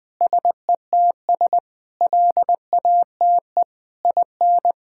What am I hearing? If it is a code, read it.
Morse code, heard as a keyed tone, about 20 wpm, 700 Hz. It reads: SETS LATE IN
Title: Sentences with 8-Word Phrases - 20wpm